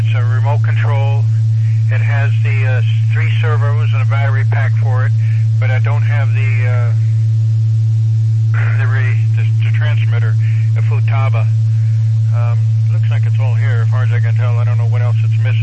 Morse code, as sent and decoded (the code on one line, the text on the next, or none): none